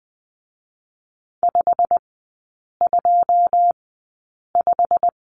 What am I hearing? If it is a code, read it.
Morse code, heard as a keyed tone, about 19 wpm, spaced as for 8 wpm, 690 Hz. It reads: 525
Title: 087 Lesson 15e - Number 5 - Three Characters 20wpm